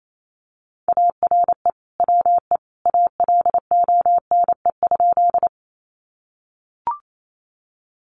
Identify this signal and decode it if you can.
Morse code, heard as a keyed tone, about 28 wpm, 700 Hz. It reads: ARE WE ALONE?
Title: Sentences from Top 1000 Words 28wpm